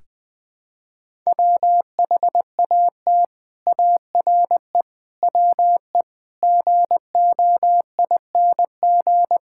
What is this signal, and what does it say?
Morse code, heard as a keyed tone, about 20 wpm, 700 Hz. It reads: WHAT ARE WE GOING
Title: Sentences with 6-Word Phrases - 20wpm